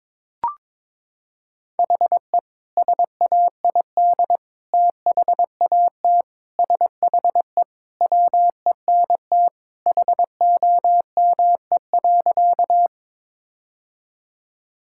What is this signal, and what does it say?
Morse code, heard as a keyed tone, about 22 wpm, 700 Hz. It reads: HE SAID THAT SHE WENT HOME.
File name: practice / 22wpm / Sentences with 2-Word Phrases - 22wpm